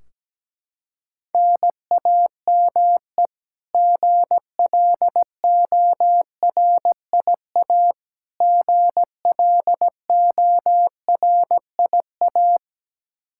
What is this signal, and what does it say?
Morse code, heard as a keyed tone, about 17 wpm, 700 Hz. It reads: NAME GLORIA GLORIA
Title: K1USN SST QSO Practice 17wpm